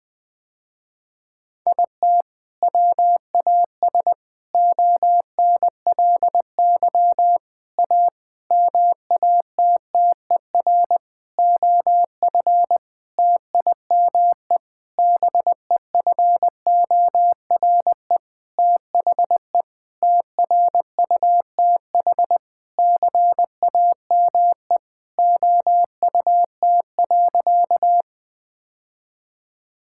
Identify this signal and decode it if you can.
Morse code, heard as a keyed tone, about 20 wpm, 700 Hz. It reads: IT WAS ONLY A MATTER OF TIME BEFORE THE TRUTH CAME OUT.
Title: Sentences with 9-Word Phrases - Spoken Beginning - 20wpm